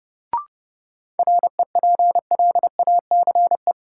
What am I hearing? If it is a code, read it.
Morse code, heard as a keyed tone, about 30 wpm, 700 Hz. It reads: REPLACE
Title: Sentences from Top 1400 Words - New Word Drill Alt - 30wpm